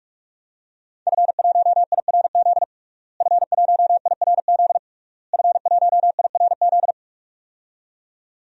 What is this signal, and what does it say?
Morse code, heard as a keyed tone, about 41 wpm, 700 Hz. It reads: F1IRZ F1IRZ F1IRZ
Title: DXCC-100 Call Signs - 5 Characters - Repeated 3x - 45wpm